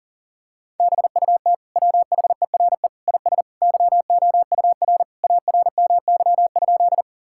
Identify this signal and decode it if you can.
Morse code, heard as a keyed tone, about 40 wpm, 700 Hz. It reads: BUT WHERE IS YOUR ARMY?
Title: Sentences from Top 1000 Words - Review Part 5 - 40wpm